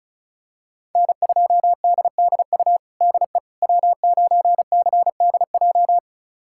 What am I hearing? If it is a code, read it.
Morse code, heard as a keyed tone, about 36 wpm, 700 Hz. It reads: N2DDU DE W9CDJ
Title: USA Callsign DE Callsign - 5 Characters - 35wpm